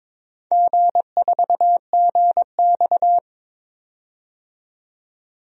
Morse code, heard as a keyed tone, about 22 wpm, 700 Hz. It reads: G4GX
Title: DXCC-100 Call Signs - 4 Characters - 22wpm